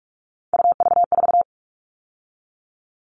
Morse code, heard as a keyed tone, about 44 wpm, 700 Hz. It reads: UV4
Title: All Cut Numbers - 3-Digit 45wpm